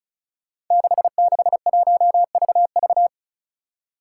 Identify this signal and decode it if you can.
Morse code, heard as a keyed tone, about 36 wpm, 700 Hz. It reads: 661VV
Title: Five Challenging Letter-Number Combinations 35wpm